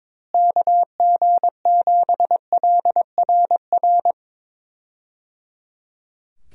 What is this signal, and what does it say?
Morse code, heard as a keyed tone, about 22 wpm, 700 Hz. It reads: KG7LRR
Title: USA Call Signs - 6 Characters - Repeated 3x - 22wpm